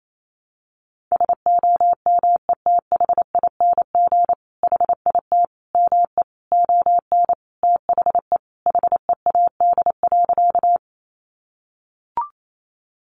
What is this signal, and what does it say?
Morse code, heard as a keyed tone, about 28 wpm, 700 Hz. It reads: SOMETHING HIT ME ON THE HEAD.
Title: Sentences from Top 1200 Words 28wpm